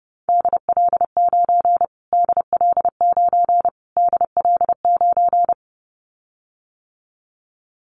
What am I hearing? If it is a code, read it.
Morse code, heard as a keyed tone, about 30 wpm, 700 Hz. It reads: DL9 DL9 DL9
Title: Callsign Prefixes for Top 25 DXCC - Repeated 3x - 30wpm